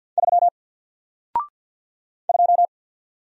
Morse code, heard as a keyed tone, about 50 wpm, 700 Hz. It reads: 3 2
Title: Single Character (Letter:Number:Punctuation:Prosign) 50wpm